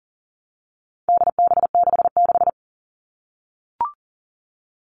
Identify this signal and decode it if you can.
Morse code, heard as a keyed tone, about 40 wpm, 700 Hz. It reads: DB66
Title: All Cut Numbers - 4-Digit 40wpm